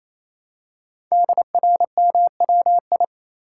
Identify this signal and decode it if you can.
Morse code, heard as a keyed tone, about 28 wpm, 700 Hz. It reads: DRMWI
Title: Code Groups - 5 Letters - 28wpm